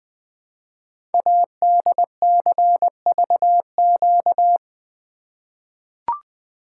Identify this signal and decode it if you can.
Morse code, heard as a keyed tone, about 20 wpm, 700 Hz. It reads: ADCVQ
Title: Code Groups - 5 Letters - 20wpm